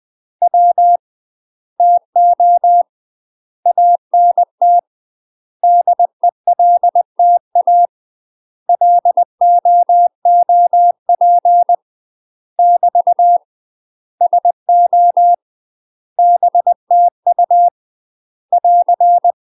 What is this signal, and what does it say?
Morse code, heard as a keyed tone, about 20 wpm, 700 Hz. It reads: W TO ANT DELTA LOOP <BT> SO BTU <AR>
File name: UK MASTER QSOs TRAINING/20WPM